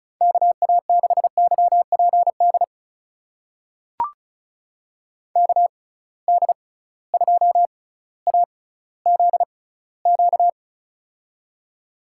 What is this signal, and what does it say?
Morse code, heard as a keyed tone, about 34 wpm, spaced as for 18 wpm, 700 Hz. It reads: KA6YPD KD2AZQ
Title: USA Call Signs - 6 Characters - 6x Character Spacing - 35wpm